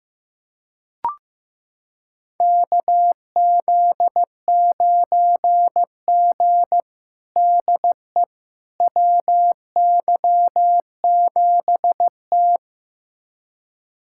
Morse code, heard as a keyed tone, about 15 wpm, 700 Hz. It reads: KZ9G DE WY7T
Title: USA Callsign DE Callsign - 4 Characters - 15wpm